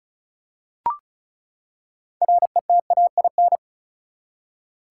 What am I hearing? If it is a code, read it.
Morse code, heard as a keyed tone, about 36 wpm, 700 Hz. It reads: RETAIN
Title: Six Letter Words - 35wpm